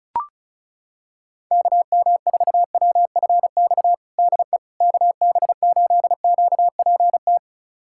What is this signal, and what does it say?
Morse code, heard as a keyed tone, about 34 wpm, 700 Hz. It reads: KM4WFX DE KB8QPT
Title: USA Callsign DE Callsign - 6 Characters - 35wpm